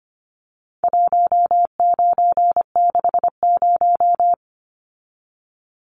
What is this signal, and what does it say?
Morse code, heard as a keyed tone, about 25 wpm, 700 Hz. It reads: 1960
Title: Numbers - 4-Digit 25wpm